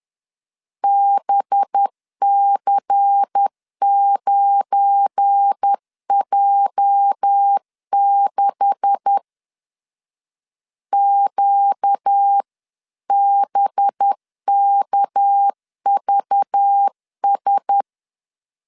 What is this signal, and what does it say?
Morse code, heard as a keyed tone, about 10 wpm, 800 Hz. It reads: BC9J6 QBKVS
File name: CW / DET / 1 9 1500C S B 10wpm 4439